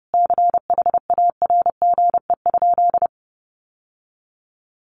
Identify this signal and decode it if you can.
Morse code, heard as a keyed tone, about 30 wpm, 700 Hz. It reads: CHARGE?
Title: Sentences from Top 900 Words 30wpm